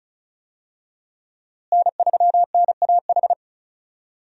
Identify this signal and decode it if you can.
Morse code, heard as a keyed tone, about 36 wpm, 700 Hz. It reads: N3NAH
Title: USA Call Signs - 5 Characters - Repeated 2x - 35wpm